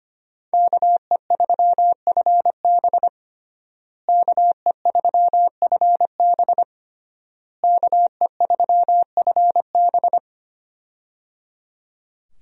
Morse code, heard as a keyed tone, about 25 wpm, 700 Hz. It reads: KE3FB KE3FB KE3FB
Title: USA Call Signs - 5 Characters - Repeated 3x - 25wpm